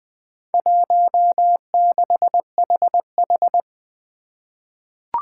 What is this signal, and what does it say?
Morse code, heard as a keyed tone, about 20 wpm, 700 Hz. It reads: 16HH
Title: Four Challenging Letter-Number Combinations 20wpm